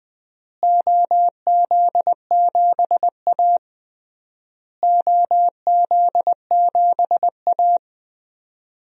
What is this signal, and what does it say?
Morse code, heard as a keyed tone, about 20 wpm, 700 Hz. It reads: OZ7A OZ7A
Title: DXCC-100 Call Signs - 4 Characters - Repeated 3x - 20wpm